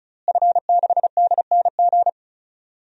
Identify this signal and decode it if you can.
Morse code, heard as a keyed tone, about 36 wpm, 700 Hz. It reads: F6DNG
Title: DXCC-100 Call Signs - 5 Characters - Repeated 3x - 35wpm